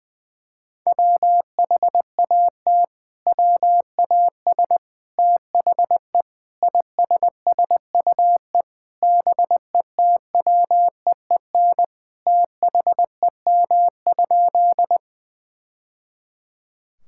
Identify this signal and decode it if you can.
Morse code, heard as a keyed tone, about 20 wpm, 700 Hz. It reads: WHAT WAS THE ISSUE BETWEEN THEM?
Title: Sentences from Top 1000 Words - Review Part 6 - 20wpm